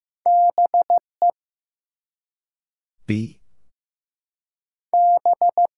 Morse code, heard as a keyed tone, about 15 wpm, 700 Hz. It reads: BE B
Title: Two Letter Words - 15wpm